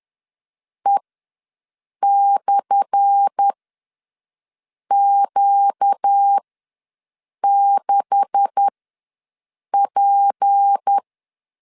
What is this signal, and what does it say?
Morse code, heard as a keyed tone, about 10 wpm, spaced as for 6 wpm, 800 Hz. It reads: E/Q6P
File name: LEZIONI / 10 SIGNS / 10B INT ws 10wpm 4134